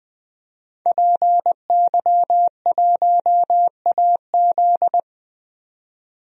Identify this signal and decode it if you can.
Morse code, heard as a keyed tone, about 20 wpm, 700 Hz. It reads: PY1AZ
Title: DXCC-100 Call Signs - 5 Characters - 20wpm